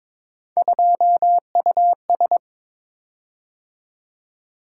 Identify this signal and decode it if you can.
Morse code, heard as a keyed tone, about 22 wpm, 700 Hz. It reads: 2US
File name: practice / 22wpm / Three Letter-Number Combinations - 22wpm